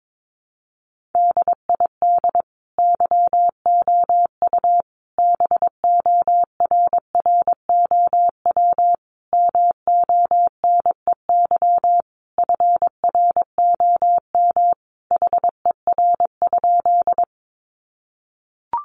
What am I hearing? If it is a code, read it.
Morse code, heard as a keyed tone, about 22 wpm, 700 Hz. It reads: DID YOU BORROW MONEY FROM HER?
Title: Sentences from Top 1500 Words 22wpm